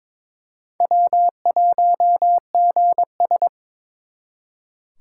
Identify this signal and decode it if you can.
Morse code, heard as a keyed tone, about 26 wpm, 700 Hz. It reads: W1GS
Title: Speed Racing - USA 1x2 Call Signs 22wpm